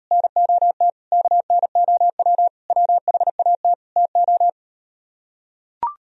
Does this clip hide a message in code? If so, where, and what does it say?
Yes: Morse code from 0.1 to 4.5 s, heard as a keyed tone, about 39 wpm, 700 Hz. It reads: NOT KNOW WHAT TO